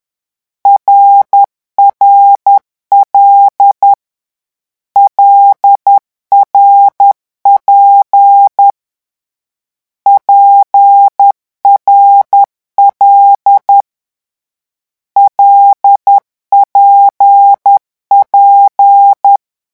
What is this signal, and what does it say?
Morse code, heard as a keyed tone, about 10 wpm, 800 Hz. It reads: RRL LRP PRL LPP
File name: morse mp3 / lecon6-vitesse 10